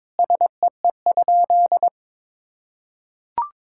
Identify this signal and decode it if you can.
Morse code, heard as a keyed tone, about 22 wpm, 700 Hz. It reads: SEE?
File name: practice / 22wpm / Sentences with 3-Word Phrases - 22wpm